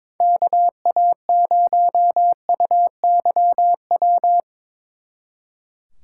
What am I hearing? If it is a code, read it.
Morse code, heard as a keyed tone, about 25 wpm, 700 Hz. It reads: KA0UYW
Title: Speed Racing - USA 2x3 Call Signs 22wpm